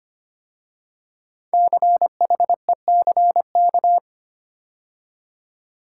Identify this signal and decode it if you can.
Morse code, heard as a keyed tone, about 25 wpm, 700 Hz. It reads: CHECK
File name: 5 Letter Words - First 3 Letters Given - 25wpm